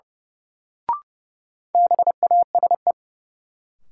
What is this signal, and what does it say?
Morse code, heard as a keyed tone, about 30 wpm, 700 Hz. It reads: BASE
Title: Sentences from Top 1000 Words - Review Part 1 - New Word Drill - 30wpm